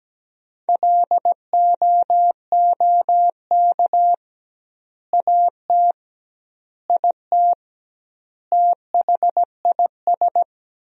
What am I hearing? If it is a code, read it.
Morse code, heard as a keyed tone, about 17 wpm, 700 Hz. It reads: LOOK AT IT THIS
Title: 2x Word-Spacing - Sentences from Top 100 Words - 17wpm